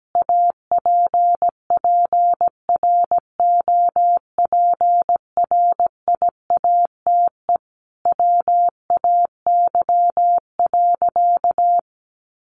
Morse code, heard as a keyed tone, about 17 wpm, 700 Hz. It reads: APPROPRIATE WAY.
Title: Sentences from Top 900 Words 17wpm